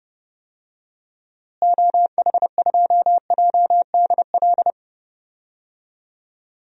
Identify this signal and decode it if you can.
Morse code, heard as a keyed tone, about 30 wpm, 700 Hz. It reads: OH2JDL
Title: DXCC-100 Call Signs - 6 Characters - Repeated 2x - 30wpm